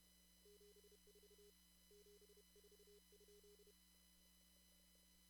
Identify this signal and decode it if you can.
Morse code, heard as a keyed tone, about 30 wpm, 410 Hz. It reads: 74 74?
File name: CW / kt5j wpx cw-op-k5pi